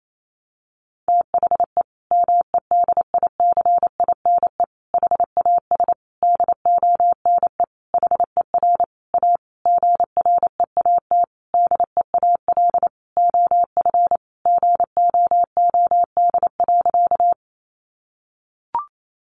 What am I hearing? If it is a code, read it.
Morse code, heard as a keyed tone, about 28 wpm, 700 Hz. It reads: THE MEDICINE HAS DONE HER A GREAT DEAL OF GOOD.
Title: Sentences from Top 900 Words 28wpm